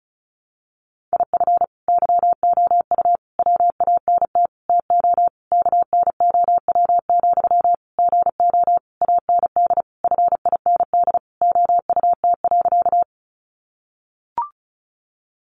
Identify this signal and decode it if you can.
Morse code, heard as a keyed tone, about 34 wpm, 700 Hz. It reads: IF YOU WANT TO KNOW, GO AND FIND OUT.
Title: Sentences from Top 200 Words 35wpm